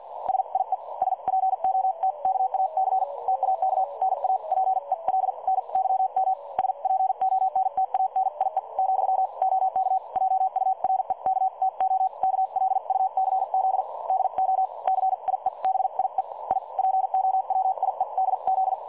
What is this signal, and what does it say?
Morse code, heard as a keyed tone, about 50 wpm, spaced as for 42 wpm, 760 Hz. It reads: SIE SPOTKAC TAK NA PELNYM SPONTANIE , OK JUREK TO KLUCZ DO CIEBIE I POWIEDZ
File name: dlazaawansowanych / nagraniazpasma / websdr recording 3565.0kHz